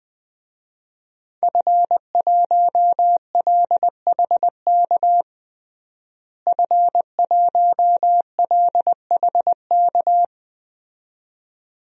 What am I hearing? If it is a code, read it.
Morse code, heard as a keyed tone, about 21 wpm, 700 Hz. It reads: F1LHK F1LHK
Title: DXCC-100 Call Signs - 5 Characters - Repeated 2x - 20wpm